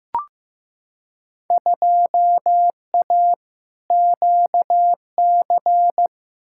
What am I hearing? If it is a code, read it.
Morse code, heard as a keyed tone, about 15 wpm, 700 Hz. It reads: 2A QC
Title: Field Day Exchanges - 15wpm